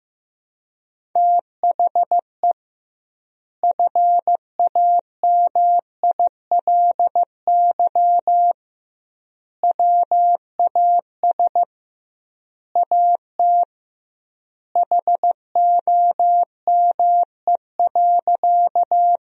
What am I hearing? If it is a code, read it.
Morse code, heard as a keyed tone, about 15 wpm, 700 Hz. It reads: THE FAMILY WAS AT HOME.
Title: 2x Word-Spacing - Sentences from Top 400 Words - 15wpm